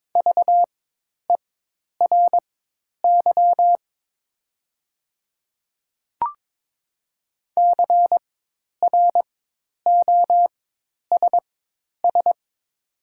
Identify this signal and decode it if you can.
Morse code, heard as a keyed tone, about 22 wpm, spaced as for 10 wpm, 700 Hz. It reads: VERY CROSS
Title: 4x Character Spacing - Top 500 Words - 22wpm